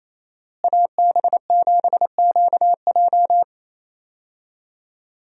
Morse code, heard as a keyed tone, about 28 wpm, 700 Hz. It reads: AB7QJ
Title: USA Call Signs - 5 Characters - Repeated 2x - 28wpm